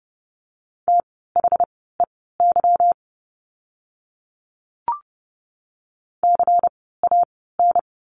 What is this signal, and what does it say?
Morse code, heard as a keyed tone, about 30 wpm, spaced as for 17 wpm, 700 Hz. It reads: THEY CAN
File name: course / 30:10wpm / 237 Lesson 38f - Letter q - Words 30wpm